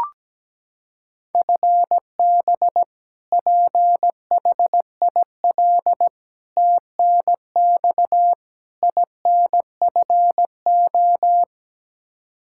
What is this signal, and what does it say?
Morse code, heard as a keyed tone, about 17 wpm, 700 Hz. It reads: FB PHIL TNX INFO
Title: Practice QSO - Exchange 3 - More Info - 17wpm